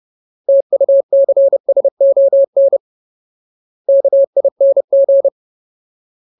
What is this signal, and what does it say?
Morse code, heard as a keyed tone, about 30 wpm, 550 Hz. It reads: TUCSON KING
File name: Ultimate Word List / 30WPM Ultimate